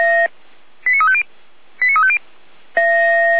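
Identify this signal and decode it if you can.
Five-tone selective call (ZVEI), heard as quick runs of five tones, tones of about 70 ms. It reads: 89350 89350
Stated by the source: ZVEI five-tone selective call received on an SDR.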